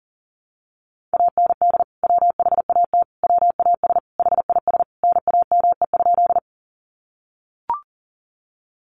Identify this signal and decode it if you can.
Morse code, heard as a keyed tone, about 40 wpm, 700 Hz. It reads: AND WHAT WAS HIS NAME?
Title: Sentences from Top 200 Words 40wpm